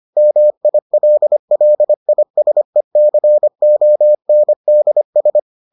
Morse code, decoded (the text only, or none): MILLISECONDS